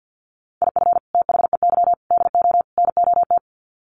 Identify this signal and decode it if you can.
Morse code, heard as a keyed tone, about 50 wpm, 700 Hz. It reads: IF THEY DO NOT
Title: Sets of 4 Words 50wpm